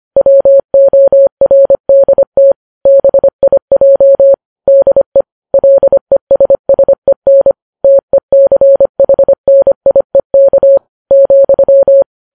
Morse code, heard as a keyed tone, about 25 wpm, 550 Hz. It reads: WORDT BIJ DE LESSEN TECHNIEK,
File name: Files Web tls / Morse on9fv / morse25kps